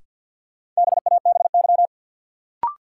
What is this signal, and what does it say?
Morse code, heard as a keyed tone, about 52 wpm, 700 Hz. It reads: BABY
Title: Sentences from Top 1300 Words - New Word Drill - 50wpm